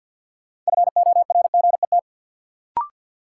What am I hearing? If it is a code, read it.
Morse code, heard as a keyed tone, about 48 wpm, 700 Hz. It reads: FORGET
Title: Sentences from Top 1200 Words - New Word Drill Alt - 50wpm